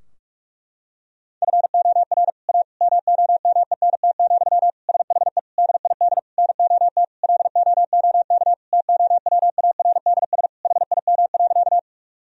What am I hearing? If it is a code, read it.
Morse code, heard as a keyed tone, about 42 wpm, 700 Hz. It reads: FOR A MOMENT, SHE DID NOT LOOK TOWARDS HIM.
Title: Sentences from Top 1000 Words - Review Part 2 - 45wpm